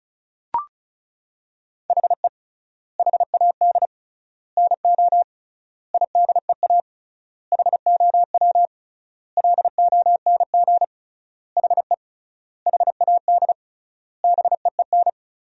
Morse code, heard as a keyed tone, about 36 wpm, 700 Hz. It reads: HE HAD NO IDEA HOW LONG HE HAD BEEN
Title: 3x Word-Spacing - Sentences from Top 300 Words - 35wpm